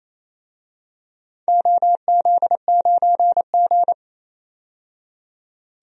Morse code, heard as a keyed tone, about 28 wpm, 700 Hz. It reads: OZ9G
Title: DXCC-100 Call Signs - 4 Characters - 28wpm